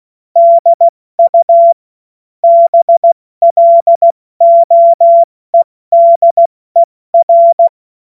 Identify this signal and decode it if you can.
Morse code, heard as a keyed tone, about 16 wpm, 680 Hz. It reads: DU BLOEDER
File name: Witz-08-Juli-2025-70-F80